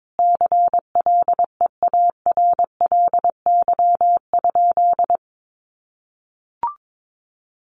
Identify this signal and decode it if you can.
Morse code, heard as a keyed tone, about 22 wpm, 700 Hz. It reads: CLEARLY?
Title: Sentences from Top 1300 Words 22wpm